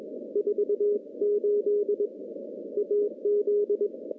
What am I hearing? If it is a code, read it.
Morse code, heard as a keyed tone, about 20 wpm, 410 Hz. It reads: 48 AZ
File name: contests / CW-SS-k5tr- -0600